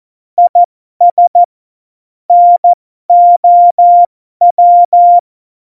Morse code, heard as a keyed tone, about 14 wpm, spaced as for 12 wpm, 700 Hz. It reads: IS NOW
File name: Morsekurs / Witze